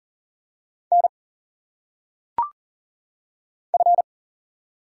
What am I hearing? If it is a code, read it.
Morse code, heard as a keyed tone, about 41 wpm, 700 Hz. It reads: N F
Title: Single Letters - Phonetic Pronunciation 40wpm